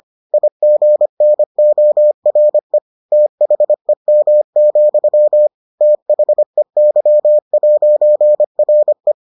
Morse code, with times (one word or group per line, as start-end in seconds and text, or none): 0.3-2.8 IGNORE
3.1-5.5 THEM,
5.8-9.1 THEY'RE